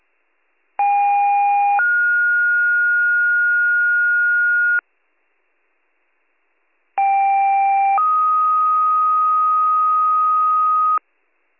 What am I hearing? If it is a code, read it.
Two-tone page, heard as a one-second tone, then a three-second tone, about 810 then 1430 Hz, about 790 then 1250 Hz. It is heard from 0.8 to 11.0 s.